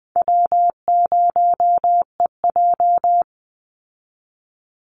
Morse code, heard as a keyed tone, about 20 wpm, 700 Hz. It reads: W0EJ
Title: Easy Call Signs Repeated 3x - 20wpm